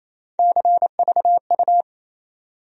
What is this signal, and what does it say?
Morse code, heard as a keyed tone, about 28 wpm, 700 Hz. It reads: CVU
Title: USA Call Sign Postfixes - 3 Letters 28wpm